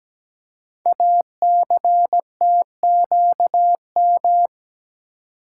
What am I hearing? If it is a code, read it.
Morse code, heard as a keyed tone, about 17 wpm, 700 Hz. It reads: ACTQM
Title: Code Groups - 5 Letters - 17wpm